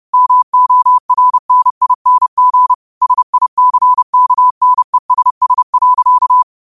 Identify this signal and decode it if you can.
Morse code, heard as a keyed tone, about 30 wpm, 1000 Hz. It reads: MORNING SICKNESS.